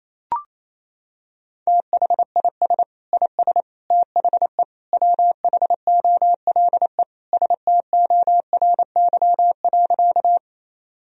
Morse code, heard as a keyed tone, about 28 wpm, 700 Hz. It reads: THIS IS THE WHOLE STORY.
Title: Sentences from Top 1000 Words - Review Part 1 - 28wpm